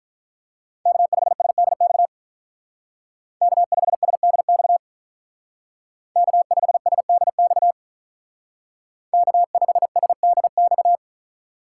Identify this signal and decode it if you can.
Morse code, heard as a keyed tone, about 43 wpm, 700 Hz. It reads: K5SDX K5SDX K5SDX K5SDX
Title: Speed Racing - USA 1x3 Call Signs 35wpm